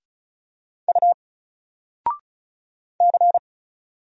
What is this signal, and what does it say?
Morse code, heard as a keyed tone, about 35 wpm, 700 Hz. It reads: U C
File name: Single-Letter-Number-35wpm